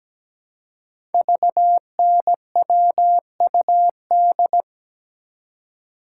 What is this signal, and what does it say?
Morse code, heard as a keyed tone, about 17 wpm, 700 Hz. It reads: VNWUD
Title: Code Groups - 5 Letters - 17wpm